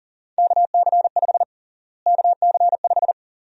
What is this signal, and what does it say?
Morse code, heard as a keyed tone, about 40 wpm, 700 Hz. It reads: KC5 KC5
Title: USA Call Sign Prefixes - Repeated 2x - 40wpm